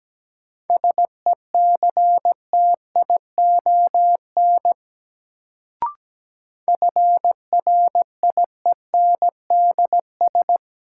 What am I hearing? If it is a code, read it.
Morse code, heard as a keyed tone, about 17 wpm, 700 Hz. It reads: SECTION FRIENDS
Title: Sentences from Top 1000 Words - Review Part 5 - New Word Drill - 17wpm